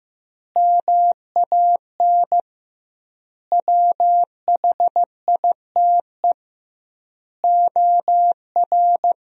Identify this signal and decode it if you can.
Morse code, heard as a keyed tone, about 15 wpm, 700 Hz. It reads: MAN WHITE OR
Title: 2x Word-Spacing - Sentences from Top 400 Words - 15wpm